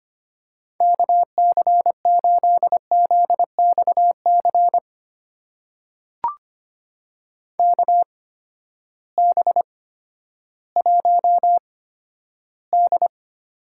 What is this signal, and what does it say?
Morse code, heard as a keyed tone, about 25 wpm, spaced as for 12 wpm, 700 Hz. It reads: KC8ZXC KB1D
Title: USA Call Signs - 6 Characters - 8x Character Spacing - 25wpm